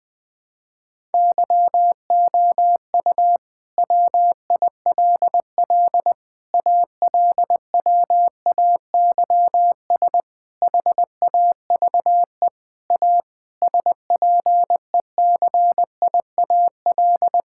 Morse code, heard as a keyed tone, about 20 wpm, 700 Hz. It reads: YOU WILL ALWAYS HAVE A SPECIAL
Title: Sentences with 9-Word Phrases - 20wpm